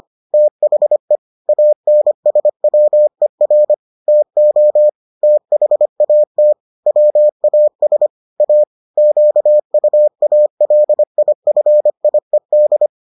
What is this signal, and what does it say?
Morse code, heard as a keyed tone, about 25 wpm, 600 Hz. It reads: THE ANSWER TO THAT WAS A QUALIFIED